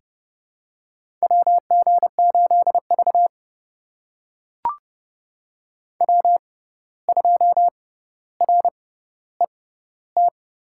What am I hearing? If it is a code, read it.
Morse code, heard as a keyed tone, about 30 wpm, spaced as for 14 wpm, 700 Hz. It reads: WG8V W2RET
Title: USA Call Signs - 6x Character Spacing - 30wpm